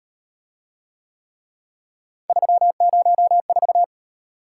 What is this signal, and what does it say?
Morse code, heard as a keyed tone, about 33 wpm, 700 Hz. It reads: 304